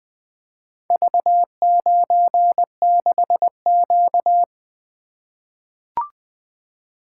Morse code, heard as a keyed tone, about 20 wpm, 700 Hz. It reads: V96Q
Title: Four Letter-Number Combinations - 20wpm